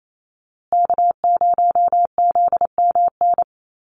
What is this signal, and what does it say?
Morse code, heard as a keyed tone, about 28 wpm, 700 Hz. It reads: K0ZMN
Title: USA Call Signs Repeated 2x 28wpm